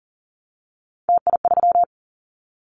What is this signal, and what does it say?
Morse code, heard as a keyed tone, about 40 wpm, 700 Hz. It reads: TI3
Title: Callsign Prefixes for Top 100 DXCC - 40wpm